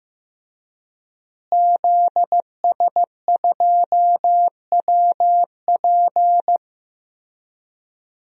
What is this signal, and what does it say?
Morse code, heard as a keyed tone, about 15 wpm, 700 Hz. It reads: ZS2WP
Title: DXCC-100 Call Signs - 5 Characters - Repeated 3x - 15wpm